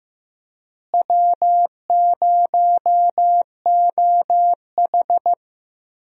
Morse code, heard as a keyed tone, about 15 wpm, 700 Hz. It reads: W0OH
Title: USA Call Signs - 4 Characters - Repeated 3x - 15wpm